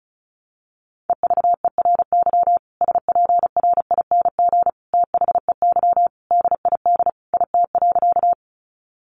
Morse code, heard as a keyed tone, about 34 wpm, 700 Hz. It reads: EVERY SPRING THEY DID IT.
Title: Sentences from Top 1200 Words 35wpm